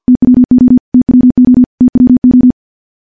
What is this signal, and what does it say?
Morse code, heard as a keyed tone, about 50 wpm, 250 Hz. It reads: TWO TWO TWO
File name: morse-training-50WPM